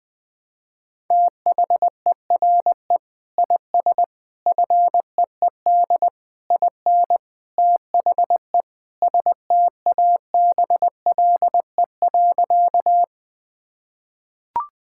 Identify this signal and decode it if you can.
Morse code, heard as a keyed tone, about 20 wpm, 700 Hz. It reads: THERE IS FEED IN THE STABLE.
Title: Sentences from Top 1600 Words 20wpm